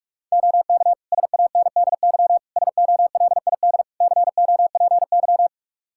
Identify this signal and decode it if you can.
Morse code, heard as a keyed tone, about 43 wpm, 700 Hz. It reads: OK SANDY SOLID COPY
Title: K1USN SST QSO Practice 45wpm